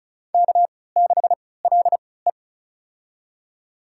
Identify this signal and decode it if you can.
Morse code, heard as a keyed tone, about 34 wpm, spaced as for 20 wpm, 700 Hz. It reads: K6LE
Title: USA Call Signs - 4 Characters - 3x Character Spacing - 35wpm